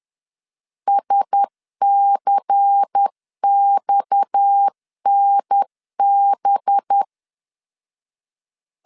Morse code, heard as a keyed tone, about 10 wpm, 800 Hz. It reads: SCXNB
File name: CW / DET / 1 6 1500C S C 10wpm 4252